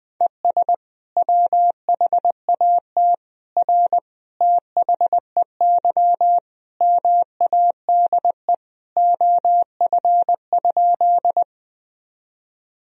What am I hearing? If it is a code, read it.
Morse code, heard as a keyed tone, about 21 wpm, 700 Hz. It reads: ES WHAT R THEY MADE OF?
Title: Sentences from Top 200 Words and Abbreviations 20wpm